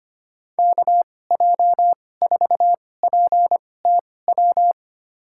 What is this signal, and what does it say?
Morse code, heard as a keyed tone, about 25 wpm, spaced as for 18 wpm, 700 Hz. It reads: KJ4PTW
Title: USA Call Signs - 6 Characters - 2x Character Spacing - 25wpm